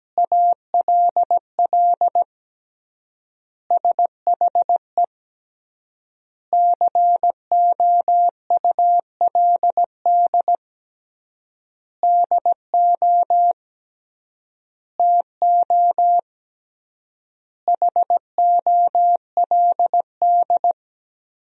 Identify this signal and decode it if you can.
Morse code, heard as a keyed tone, about 17 wpm, 700 Hz. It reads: ALL SHE COULD DO TO HOLD
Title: 3x Word-Spacing - Sentences from Top 400 Words - 17wpm